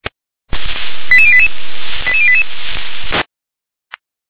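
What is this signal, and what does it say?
Five-tone selective call (ZVEI), heard as quick runs of five tones, tones of about 70 ms. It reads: 88088 88088
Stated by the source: Five-tone ZVEI selective call received on SDR.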